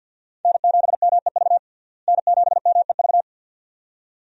Morse code, heard as a keyed tone, about 52 wpm, 700 Hz. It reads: N7MEV N7MEV
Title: USA Call Signs - 5 Characters - Repeated 2x - 50wpm